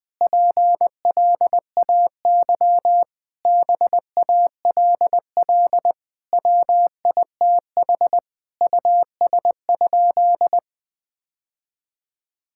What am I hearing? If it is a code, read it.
Morse code, heard as a keyed tone, about 20 wpm, 700 Hz. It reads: PLAY BALL WITH US?
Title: Sentences from Top 1000 Words - Review Part 6 - 20wpm